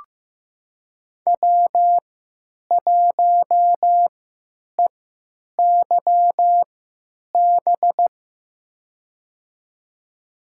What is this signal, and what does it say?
Morse code, heard as a keyed tone, about 15 wpm, spaced as for 9 wpm, 700 Hz. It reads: W1EYB
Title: USA Call Signs - 5 Characters - 3x Character Spacing - 15wpm